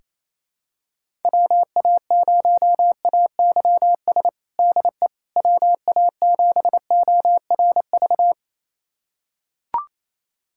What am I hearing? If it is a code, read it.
Morse code, heard as a keyed tone, about 28 wpm, 700 Hz. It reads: WA0AYS DE WA7ORV
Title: USA Callsign DE Callsign - 6 Characters - 28wpm